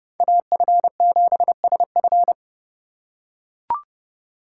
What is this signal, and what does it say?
Morse code, heard as a keyed tone, about 27 wpm, 700 Hz. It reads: AF7SF